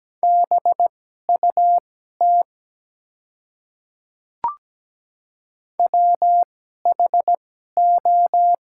Morse code, heard as a keyed tone, about 17 wpm, spaced as for 12 wpm, 700 Hz. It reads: BUT WHO
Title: 2x Character Spacing - Top 300 Words - 17wpm